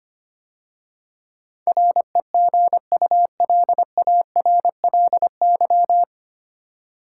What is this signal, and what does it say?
Morse code, heard as a keyed tone, about 25 wpm, 700 Hz. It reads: REGULARLY
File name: practice / 25wpm / 9 Letter Words - First 3 Letters Given - 25wpm